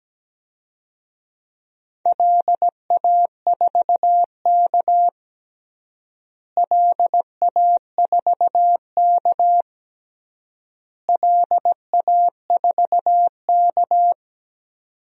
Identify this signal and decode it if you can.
Morse code, heard as a keyed tone, about 17 wpm, 700 Hz. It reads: LA4K LA4K LA4K
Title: DXCC-100 Call Signs - 4 Characters - Repeated 3x - 17wpm